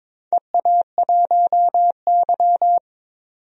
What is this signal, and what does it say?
Morse code, heard as a keyed tone, about 22 wpm, 700 Hz. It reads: EA1Y
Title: DXCC-100 Call Signs - 4 Characters - 22wpm